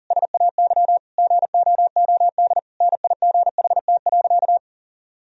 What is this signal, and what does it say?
Morse code, heard as a keyed tone, about 40 wpm, 700 Hz. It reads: SAY GOOD NIGHT.
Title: Sentences from Top 1000 Words - Review Part 4 - 40wpm